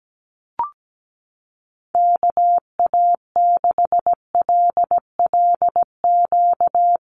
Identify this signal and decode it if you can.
Morse code, heard as a keyed tone, about 17 wpm, 700 Hz. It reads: KA6LLQ
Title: USA Call Signs Repeated 3x 17wpm